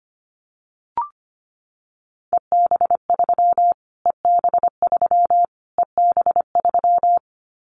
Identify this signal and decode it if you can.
Morse code, heard as a keyed tone, about 25 wpm, 700 Hz. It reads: EB3 EB3 EB3
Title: Callsign Prefixes for Top 25 DXCC - Repeated 3x - 25wpm